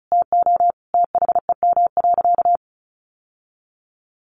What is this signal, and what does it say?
Morse code, heard as a keyed tone, about 34 wpm, 700 Hz. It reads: TO THEM.
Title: Sentences from Top 1200 Words 35wpm